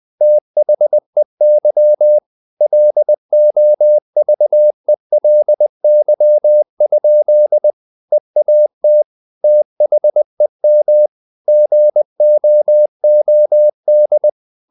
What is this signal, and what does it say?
Morse code, heard as a keyed tone, about 20 wpm, 600 Hz. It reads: THEY LOVELY? EAT THEM GOOD
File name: the old man and the sea / the old man and the sea part 1 20wpm